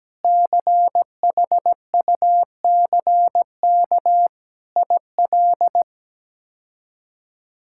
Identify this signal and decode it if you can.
Morse code, heard as a keyed tone, about 17 wpm, 700 Hz. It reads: CHUCK IL
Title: K1USN SST QSO Practice - 17wpm